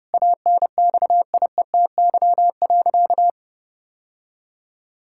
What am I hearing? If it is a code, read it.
Morse code, heard as a keyed tone, about 30 wpm, 700 Hz. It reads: ANXIETY.
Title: Sentences from Top 1600 Words 30wpm